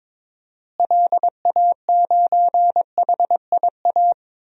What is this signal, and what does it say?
Morse code, heard as a keyed tone, about 22 wpm, 700 Hz. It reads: LA9HIA
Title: DXCC-100 Call Signs - 6 Characters - Repeated 3x - 22wpm